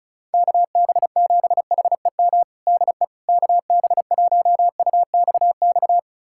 Morse code, heard as a keyed tone, about 35 wpm, 700 Hz. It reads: KB7HEM DE KB1UXX
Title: USA Callsign DE Callsign - 6 Characters - 35wpm